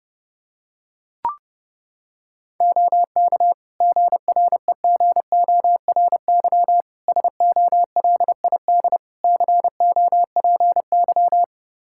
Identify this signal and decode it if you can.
Morse code, heard as a keyed tone, about 30 wpm, 700 Hz. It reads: OK GREGORY SOLID COPY
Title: Practice QSO - Exchange 3 - More Info - 30wpm